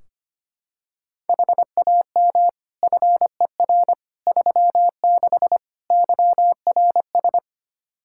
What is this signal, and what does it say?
Morse code, heard as a keyed tone, about 25 wpm, 700 Hz. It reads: HAM FER 36 YRS
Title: Practice QSO - Exchange 3 - More Info - 25wpm